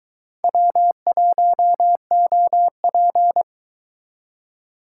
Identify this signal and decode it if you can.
Morse code, heard as a keyed tone, about 22 wpm, 700 Hz. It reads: W1OP